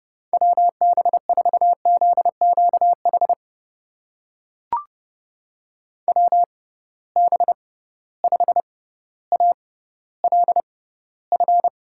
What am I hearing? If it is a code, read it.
Morse code, heard as a keyed tone, about 30 wpm, spaced as for 15 wpm, 700 Hz. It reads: WB4ZQH WB5ALF
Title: USA Call Signs - 6 Characters - 6x Character Spacing - 30wpm